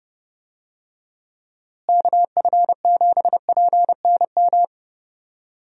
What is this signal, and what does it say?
Morse code, heard as a keyed tone, about 30 wpm, 700 Hz. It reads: KF7PNM
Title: USA Call Signs - 6 Characters - Repeated 2x - 30wpm